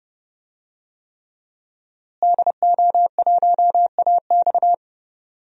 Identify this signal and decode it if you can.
Morse code, heard as a keyed tone, about 30 wpm, 700 Hz. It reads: DO1AX
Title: DXCC-100 Call Signs - 5 Characters - Repeated 3x - 30wpm